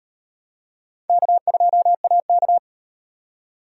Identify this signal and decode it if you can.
Morse code, heard as a keyed tone, about 38 wpm, 700 Hz. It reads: K2AK